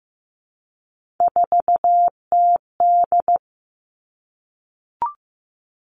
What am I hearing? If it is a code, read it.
Morse code, heard as a keyed tone, about 15 wpm, 700 Hz. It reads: 4TD
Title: Three Challenging Letter-Number Combinations 15wpm